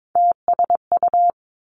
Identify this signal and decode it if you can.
Morse code, heard as a keyed tone, about 22 wpm, 700 Hz. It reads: TSU
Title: Three Challenging Letter-Number Combinations 22wpm